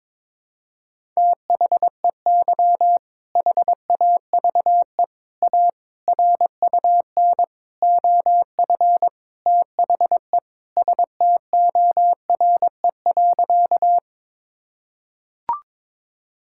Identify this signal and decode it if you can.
Morse code, heard as a keyed tone, about 22 wpm, 700 Hz. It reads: THEY HAVE A RUN OF THE STORE.
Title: Sentences from Top 1000 Words - Review Part 5 - 22wpm